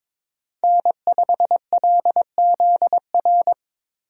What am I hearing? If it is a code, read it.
Morse code, heard as a keyed tone, about 22 wpm, 700 Hz. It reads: N5LZR
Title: USA Call Signs - 5 Characters - Repeated 3x - 22wpm